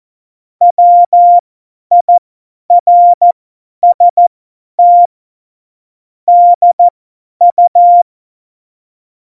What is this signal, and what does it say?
Morse code, heard as a keyed tone, about 14 wpm, spaced as for 10 wpm, 700 Hz. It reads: WIRST DU
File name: Morsekurs / Witze / Witz 15. Dezember 2020, 50 F70